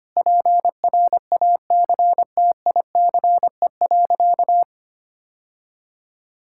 Morse code, heard as a keyed tone, about 25 wpm, 700 Hz. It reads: PRACTICE.
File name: practice / 25wpm / Sentences with 9-Word Phrases - 25wpm